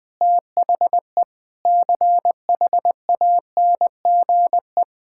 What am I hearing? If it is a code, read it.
Morse code, heard as a keyed tone, about 20 wpm, 700 Hz. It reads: THE CHANGE
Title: Sentences with 7-Word Phrases - 20wpm